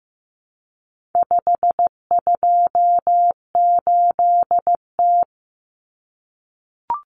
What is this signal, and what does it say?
Morse code, heard as a keyed tone, about 15 wpm, 700 Hz. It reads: 528T
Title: Common Cut Numbers - 4-Digit 15wpm